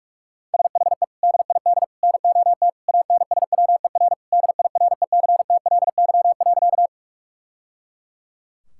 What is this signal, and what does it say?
Morse code, heard as a keyed tone, about 41 wpm, 700 Hz. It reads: SHE DID NOT ANSWER DIRECTLY.
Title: Sentences from Top 1000 Words - Review Part 5 - 45wpm